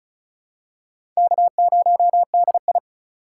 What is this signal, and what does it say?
Morse code, heard as a keyed tone, about 36 wpm, 700 Hz. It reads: K0DI
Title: USA Call Signs - 3x Character Spacing - 35wpm